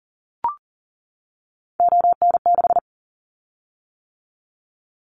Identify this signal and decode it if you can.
Morse code, heard as a keyed tone, about 38 wpm, 700 Hz. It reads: ON6
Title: Callsign Prefixes for Top 25 DXCC - 40wpm